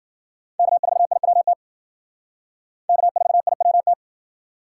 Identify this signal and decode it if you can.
Morse code, heard as a keyed tone, about 50 wpm, 700 Hz. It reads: K4IWT K4IWT
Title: Speed Racing - USA 1x3 Call Signs 40wpm